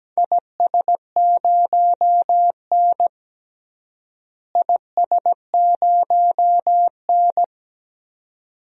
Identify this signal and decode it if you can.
Morse code, heard as a keyed tone, about 17 wpm, 700 Hz. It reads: IS0N IS0N
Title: DXCC-100 Call Signs - 4 Characters - Repeated 2x - 17wpm